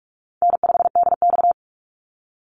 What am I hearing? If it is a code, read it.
Morse code, heard as a keyed tone, about 46 wpm, 700 Hz. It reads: N5DX
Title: Easy Call Signs - 45wpm